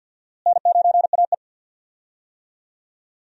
Morse code, heard as a keyed tone, about 52 wpm, 700 Hz. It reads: N9AE
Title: USA Call Signs - 4 Characters - Repeated 2x - 50wpm